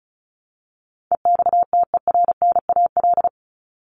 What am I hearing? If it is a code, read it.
Morse code, heard as a keyed tone, about 34 wpm, 700 Hz. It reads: EXTERNAL
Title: Top 1751-2000 Words 35wpm